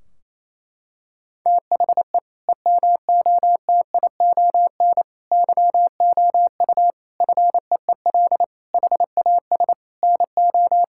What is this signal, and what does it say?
Morse code, heard as a keyed tone, about 28 wpm, 700 Hz. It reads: THE EMOTION YOU FEEL HAS NO
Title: Sentences from Top 1600 Words 28wpm